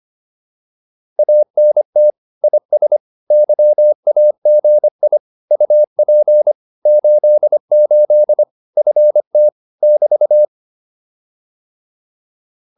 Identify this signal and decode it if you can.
Morse code, heard as a keyed tone, about 25 wpm, 600 Hz. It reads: ANT IS YAGI UP 88 FT <BT>
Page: advanced-practice-files